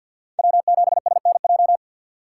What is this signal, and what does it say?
Morse code, heard as a keyed tone, about 47 wpm, 700 Hz. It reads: W7SNJ
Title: USA Call Signs - 3x Character Spacing - 50wpm